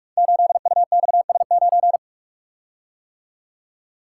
Morse code, heard as a keyed tone, about 44 wpm, 700 Hz. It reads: 8UXS9
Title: Code Groups - 5 Letter:Numbers - 45wpm